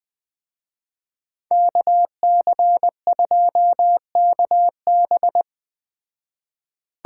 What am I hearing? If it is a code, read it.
Morse code, heard as a keyed tone, about 20 wpm, 700 Hz. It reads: KC2KB